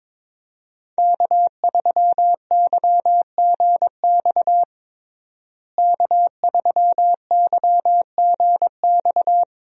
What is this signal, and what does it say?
Morse code, heard as a keyed tone, about 22 wpm, 700 Hz. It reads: K3YGX K3YGX
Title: USA Call Signs - 5 Characters - Repeated 2x - 22wpm